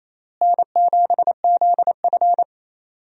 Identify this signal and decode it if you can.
Morse code, heard as a keyed tone, about 28 wpm, 700 Hz. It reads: N7ZF
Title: USA Call Signs - 4 Characters - Repeated 2x - 28wpm